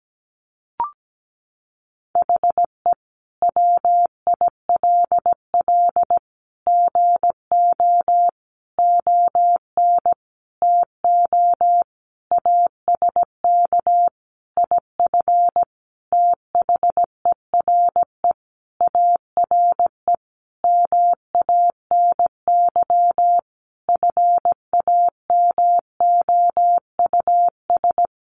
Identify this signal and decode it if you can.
Morse code, heard as a keyed tone, about 17 wpm, 700 Hz. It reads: HE WILL GO ON TO ASK IF THERE ARE MANY FAMOUS
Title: Sentences from Top 1500 Words 17wpm